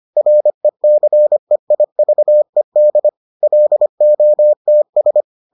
Morse code, heard as a keyed tone, about 25 wpm, 600 Hz. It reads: RECEIVED LOTS